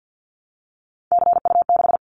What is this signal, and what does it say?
Morse code, heard as a keyed tone, about 50 wpm, 700 Hz. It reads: CU6
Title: Callsign Prefixes for Top 100 DXCC - 50wpm